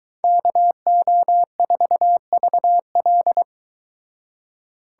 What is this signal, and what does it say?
Morse code, heard as a keyed tone, about 23 wpm, 700 Hz. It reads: KO4VL